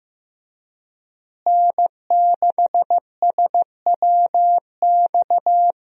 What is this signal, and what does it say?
Morse code, heard as a keyed tone, about 15 wpm, 700 Hz. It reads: N6SWX
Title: USA Call Signs - 5 Characters - Repeated 2x - 15wpm